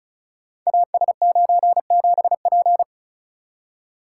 Morse code, heard as a keyed tone, about 36 wpm, 700 Hz. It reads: AS97P
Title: Code Groups - 5 Letter:Numbers - 35wpm